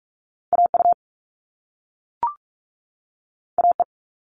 Morse code, heard as a keyed tone, about 46 wpm, 700 Hz. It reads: AU AE
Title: All Cut Numbers - 2-Digit 45wpm